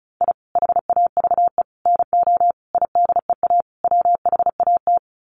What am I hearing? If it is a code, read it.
Morse code, heard as a keyed tone, about 34 wpm, 700 Hz. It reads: I HAVE NO IDEA WHAT
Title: Sets of 5 Words Encore 35wpm